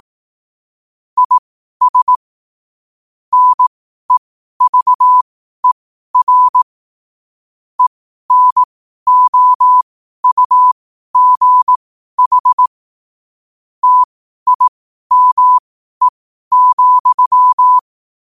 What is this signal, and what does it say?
Morse code, heard as a keyed tone, about 18 wpm, spaced as for 12 wpm, 1000 Hz. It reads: IS NEVER ENOUGH TIME,
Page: 12 wpm morse code quotes